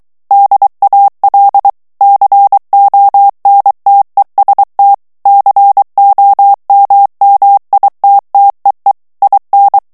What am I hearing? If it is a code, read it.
Morse code, heard as a keyed tone, about 23 wpm, 800 Hz. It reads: DAL CONTEST COMMITTEE IN
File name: ITALIEN texte clair 2 vitesse 22 mots minute